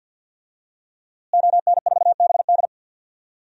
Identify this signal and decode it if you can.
Morse code, heard as a keyed tone, about 48 wpm, 700 Hz. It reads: ON4BD
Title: DXCC-100 Call Signs - 5 Characters - Repeated 2x - 50wpm